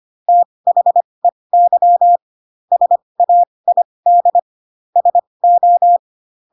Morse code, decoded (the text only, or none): THEY SAID SO